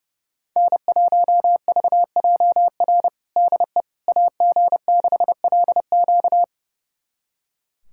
Morse code, heard as a keyed tone, about 30 wpm, 700 Hz. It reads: N1VJR DE AG6LQ
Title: USA Callsign DE Callsign - 5 Characters - 30wpm